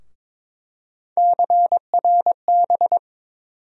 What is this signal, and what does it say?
Morse code, heard as a keyed tone, about 22 wpm, 700 Hz. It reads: CRB
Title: USA Call Sign Postfixes - 3 Letters 22wpm